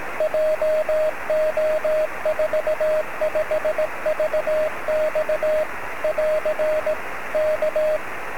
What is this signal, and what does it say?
Morse code, heard as a keyed tone, about 18 wpm, 630 Hz, with radio noise. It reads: JO45VX <AR> K
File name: sm7lcb / 25-oz9ky-23cm